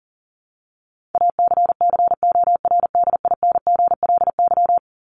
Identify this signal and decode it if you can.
Morse code, heard as a keyed tone, about 40 wpm, 700 Hz. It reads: ACCORDINGLY
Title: LY Suffix - Top 100 Words with Base Words 40wpm